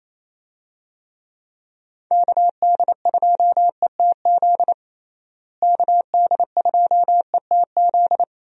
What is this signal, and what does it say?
Morse code, heard as a keyed tone, about 28 wpm, 700 Hz. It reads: KD2ETZ KD2ETZ
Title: USA Call Signs - 6 Characters - Repeated 3x - 28wpm